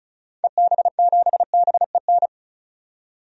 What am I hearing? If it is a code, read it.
Morse code, heard as a keyed tone, about 34 wpm, 700 Hz. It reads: EB7BEN
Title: DXCC-100 Call Signs - 6 Characters - 35wpm